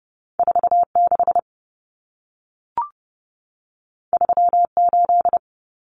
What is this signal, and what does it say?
Morse code, heard as a keyed tone, about 30 wpm, 700 Hz. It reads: 46 38
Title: Numbers - 2-Digit 30wpm